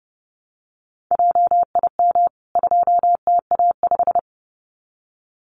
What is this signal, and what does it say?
Morse code, heard as a keyed tone, about 29 wpm, 700 Hz. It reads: JIM 2TA5
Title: CWT Exchange 30wpm